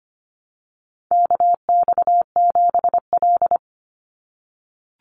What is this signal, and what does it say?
Morse code, heard as a keyed tone, about 25 wpm, 700 Hz. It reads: KX7L
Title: NA SOTA Chaser Callsigns 25wpm